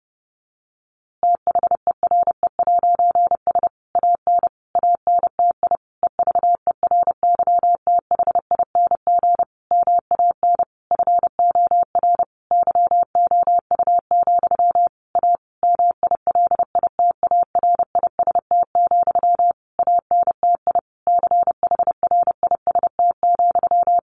Morse code, heard as a keyed tone, about 30 wpm, 700 Hz. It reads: THERE'S AN ANTI EVERYTHING MAN FOR YOU, A MILITARIST, ANTI CHRIST,
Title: A Sound Of Thunder 30wpm